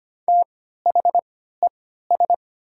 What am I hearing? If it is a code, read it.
Morse code, heard as a keyed tone, about 25 wpm, spaced as for 14 wpm, 690 Hz. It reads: THES
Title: 3x Character Spacing - Top 500 Words - 25wpm